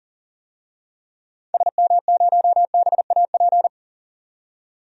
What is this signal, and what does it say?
Morse code, heard as a keyed tone, about 40 wpm, 700 Hz. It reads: SM0BAP
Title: DXCC-100 Call Signs - 6 Characters - 40wpm